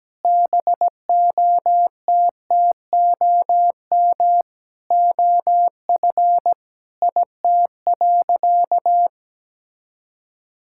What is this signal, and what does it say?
Morse code, heard as a keyed tone, about 17 wpm, 700 Hz. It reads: BOTTOM OF IT.
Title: Sentences from Top 1000 Words - Review Part 6 - 17wpm